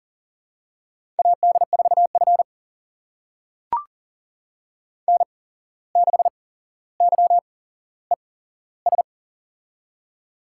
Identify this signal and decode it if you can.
Morse code, heard as a keyed tone, about 38 wpm, spaced as for 16 wpm, 700 Hz. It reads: AD4F N6YES
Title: USA Call Signs - 8x Character Spacing - 40wpm